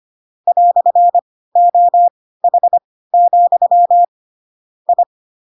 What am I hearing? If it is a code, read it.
Morse code, heard as a keyed tone, about 25 wpm, spaced as for 16 wpm, 700 Hz. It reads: "OH, I
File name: Pinocchio / Pinocchio ch1 10 25 ik1hge